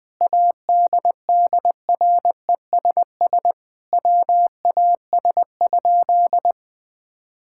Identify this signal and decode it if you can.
Morse code, heard as a keyed tone, about 20 wpm, 700 Hz. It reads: ADDRESS WAS?
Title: Sentences from Top 1000 Words - Review Part 6 - 20wpm